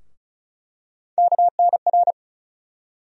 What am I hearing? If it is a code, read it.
Morse code, heard as a keyed tone, about 34 wpm, 700 Hz. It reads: KNR
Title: USA Call Sign Postfixes - 3 Letters 35wpm